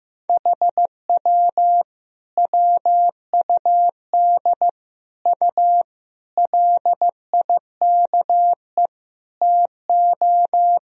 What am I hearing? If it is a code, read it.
Morse code, heard as a keyed tone, about 16 wpm, 700 Hz. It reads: HW WUD U LIKE TO
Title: Sentences from Top 100 Words and Abbreviations 15wpm